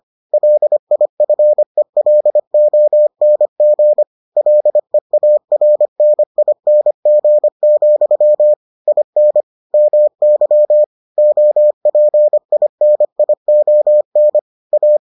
Morse code, heard as a keyed tone, about 25 wpm, 600 Hz. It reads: LIFELONG LEARNING, IN MY OPINION A